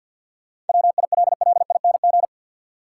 Morse code, heard as a keyed tone, about 52 wpm, 700 Hz. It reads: WILLING
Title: Sentences from Top 1000 Words - New Word Drill - 50wpm